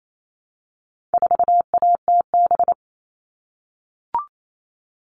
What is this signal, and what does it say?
Morse code, heard as a keyed tone, about 28 wpm, 700 Hz. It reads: 4ATB
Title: All Cut Numbers - 4-Digit 28wpm